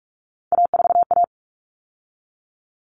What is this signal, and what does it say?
Morse code, heard as a keyed tone, about 44 wpm, 700 Hz. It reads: A4A
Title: All Cut Numbers - 3-Digit 45wpm